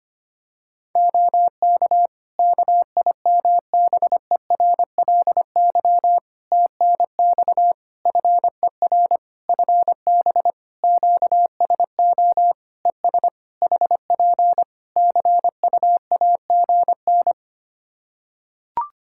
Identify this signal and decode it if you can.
Morse code, heard as a keyed tone, about 25 wpm, 700 Hz. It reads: OK KIMBERLY TNX FER FB QSO ES HP CUAGN
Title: Practice QSO - Exchange 4 - Ending - 25wpm